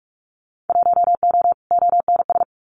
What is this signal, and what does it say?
Morse code, heard as a keyed tone, about 46 wpm, 700 Hz. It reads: 1O ONS
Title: Winter Field Day Exchanges Encore 45wpm